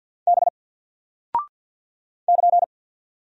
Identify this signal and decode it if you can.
Morse code, heard as a keyed tone, about 48 wpm, 700 Hz. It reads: B <KN>
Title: Single Character (Letter:Number:Punctuation:Prosign) Alt 50wpm